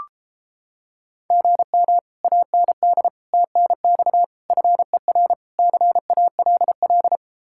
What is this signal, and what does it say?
Morse code, heard as a keyed tone, about 29 wpm, 700 Hz. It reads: GM AND TNX FER CALL